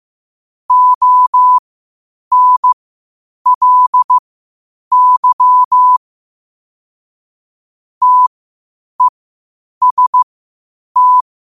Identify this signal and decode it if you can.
Morse code, heard as a keyed tone, about 15 wpm, spaced as for 8 wpm, 1000 Hz. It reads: ONLY TEST
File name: qotdfiles / 8wpm